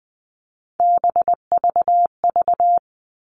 Morse code, heard as a keyed tone, about 20 wpm, 700 Hz. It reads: BVV
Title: All Cut Numbers - 3-Digit 20wpm